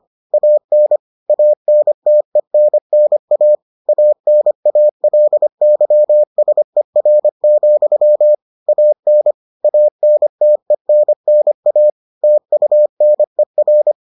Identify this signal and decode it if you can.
Morse code, heard as a keyed tone, about 25 wpm, 600 Hz. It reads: AN ANTENNA ANALYSER, AN ANTENNA TUNER